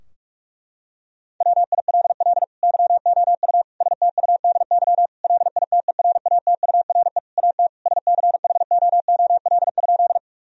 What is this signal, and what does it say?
Morse code, heard as a keyed tone, about 46 wpm, 700 Hz. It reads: WILL YOU STUDY LITERATURE AT SCHOOL?
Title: Sentences from Top 1000 Words - Review Part 3 - 45wpm